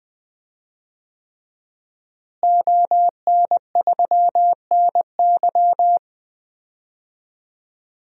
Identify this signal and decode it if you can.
Morse code, heard as a keyed tone, about 20 wpm, 700 Hz. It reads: ON3NY
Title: DXCC-100 Call Signs - 5 Characters - Repeated 3x - 20wpm